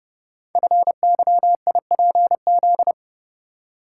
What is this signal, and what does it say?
Morse code, heard as a keyed tone, about 30 wpm, 700 Hz. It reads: FYIPZ
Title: Code Groups - 5 Letter:Numbers - 30wpm